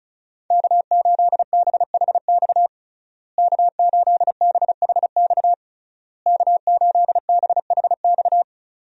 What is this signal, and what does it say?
Morse code, heard as a keyed tone, about 36 wpm, 700 Hz. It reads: K8BHX K8BHX K8BHX
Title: USA Call Signs - 5 Characters - Repeated 3x - 35wpm